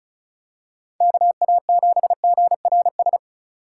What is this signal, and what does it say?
Morse code, heard as a keyed tone, about 34 wpm, 700 Hz. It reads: KA7GRS
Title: USA Call Signs - 6 Characters - 2x Character Spacing - 35wpm